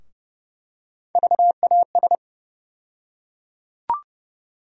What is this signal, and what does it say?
Morse code, heard as a keyed tone, about 30 wpm, 700 Hz. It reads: VAS
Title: USA Call Sign Postfixes - 3 Letters 30wpm